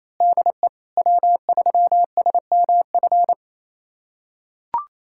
Morse code, heard as a keyed tone, about 28 wpm, 700 Hz. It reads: DE W3SMF
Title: USA Callsign DE Callsign - 5 Characters - 28wpm